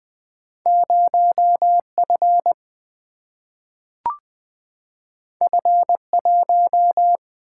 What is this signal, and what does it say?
Morse code, heard as a keyed tone, about 20 wpm, 700 Hz. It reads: 0F F1
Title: Single-Letter Single-Number 20wpm